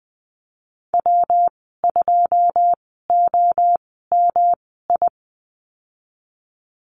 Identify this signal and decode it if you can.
Morse code, heard as a keyed tone, about 20 wpm, spaced as for 15 wpm, 700 Hz. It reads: W2OMI
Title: 186 Lesson 30g - Letter b - Call Signs 20wpm